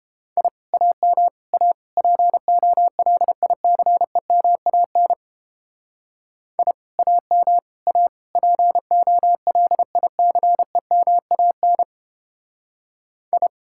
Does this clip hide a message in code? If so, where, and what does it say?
Yes: Morse code from 0.4 to 13.5 s, heard as a keyed tone, about 29 wpm, 700 Hz. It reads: I AM A POLICEMAN I AM A POLICEMAN I